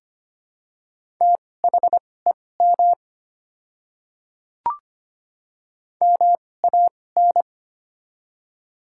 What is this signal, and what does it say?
Morse code, heard as a keyed tone, about 25 wpm, spaced as for 18 wpm, 700 Hz. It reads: THEM MAN
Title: 2x Character Spacing - Top 200 Words - 25wpm